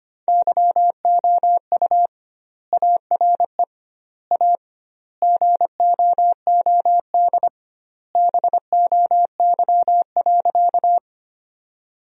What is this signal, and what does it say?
Morse code, heard as a keyed tone, about 25 wpm, 700 Hz. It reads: YOU ARE A GOOD BOY.
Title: 2x Word-Spacing - Sentences from Top 200 Words - 25wpm